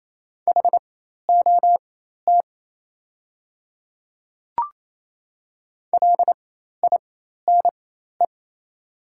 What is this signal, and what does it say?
Morse code, heard as a keyed tone, about 28 wpm, spaced as for 13 wpm, 700 Hz. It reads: HOT LINE
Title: 4x Character Spacing - Top 200 Words - 28wpm